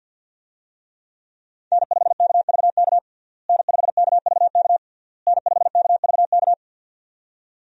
Morse code, heard as a keyed tone, about 46 wpm, 700 Hz. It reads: N5KVK N5KVK N5KVK
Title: USA Call Signs - 5 Characters - Repeated 3x - 50wpm